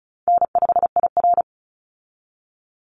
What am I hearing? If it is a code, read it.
Morse code, heard as a keyed tone, about 34 wpm, 700 Hz. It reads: N5IR
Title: Call Signs Repeated 3x - 35wpm